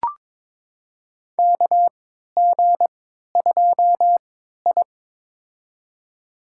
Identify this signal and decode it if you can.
Morse code, heard as a keyed tone, about 22 wpm, spaced as for 13 wpm, 700 Hz. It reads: KG2I
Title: USA Call Signs - 4 Characters - 3x Character Spacing - 22wpm